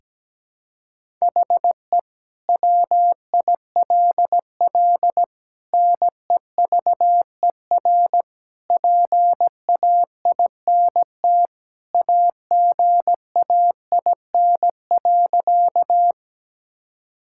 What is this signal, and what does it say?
Morse code, heard as a keyed tone, about 17 wpm, 700 Hz. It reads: HE WILL NEVER PAINT AGAIN.
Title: Sentences from Top 1000 Words - Review Part 6 - 17wpm